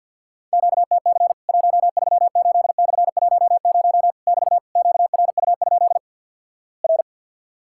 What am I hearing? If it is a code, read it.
Morse code, heard as a keyed tone, about 50 wpm, 700 Hz. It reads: QTC 138/10 <BT> QRU? R
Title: Worked All Europe DX Contest - QTC Exchange - 50wpm